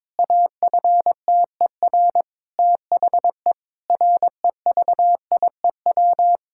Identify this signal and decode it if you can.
Morse code, heard as a keyed tone, about 22 wpm, 700 Hz. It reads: AFTER THE REVIEW
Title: Sentences from Top 1000 Words - Review Part 6 - 22wpm